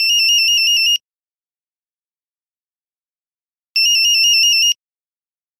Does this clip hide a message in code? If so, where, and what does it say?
No.